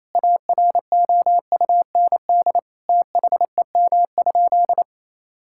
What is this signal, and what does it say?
Morse code, heard as a keyed tone, about 28 wpm, 700 Hz. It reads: AROUND THEM?
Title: Sentences from Top 1000 Words - Review Part 5 - 28wpm